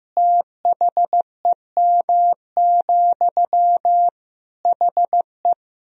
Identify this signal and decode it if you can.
Morse code, heard as a keyed tone, about 15 wpm, 700 Hz. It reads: THEM, HE
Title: Sentences from Top 400 Words and Abbreviations 15wpm